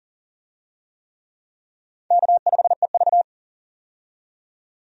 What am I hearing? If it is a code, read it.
Morse code, heard as a keyed tone, about 40 wpm, 700 Hz. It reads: K5EV
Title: USA Call Signs - 4 Characters - 40wpm